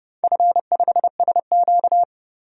Morse code, heard as a keyed tone, about 32 wpm, 700 Hz. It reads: F5SQ
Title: DXCC-100 Call Signs - 4 Characters - 30wpm